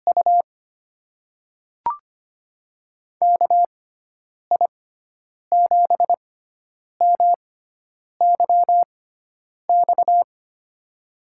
Morse code, heard as a keyed tone, about 25 wpm, spaced as for 9 wpm, 700 Hz. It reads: U KI7MYX
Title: USA Call Signs - 6 Characters - 6x Character Spacing - 25wpm